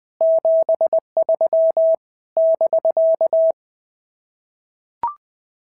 Morse code, heard as a keyed tone, about 20 wpm, 650 Hz. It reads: 73 <BK>
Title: POTA QSO Practice - 20wpm